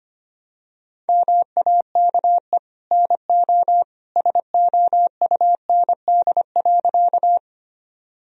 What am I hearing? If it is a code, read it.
Morse code, heard as a keyed tone, about 25 wpm, 700 Hz. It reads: MAKE NO SOUND.
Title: Sentences from Top 1000 Words - Review Part 1 - 25wpm